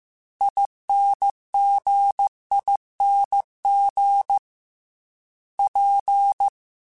Morse code, heard as a keyed tone, about 15 wpm, 800 Hz. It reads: INGING P
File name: Special INGs / Special INGs-mp3s / Dble Ings sp @13